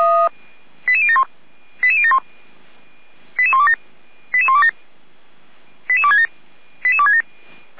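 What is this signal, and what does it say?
Five-tone selective call (ZVEI), heard as quick runs of five tones, tones of about 70 ms. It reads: 88071 88071 88127 88127 88267 88267
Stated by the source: ZVEI five-tone call received on SDR, from the feuerwehr-orlamuende.org fire brigade audio archive.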